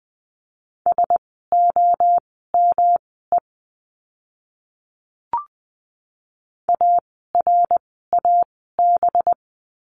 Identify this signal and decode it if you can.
Morse code, heard as a keyed tone, about 20 wpm, spaced as for 14 wpm, 700 Hz. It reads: SOME ARAB
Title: 185 Lesson 30f - Letter b - Words 20wpm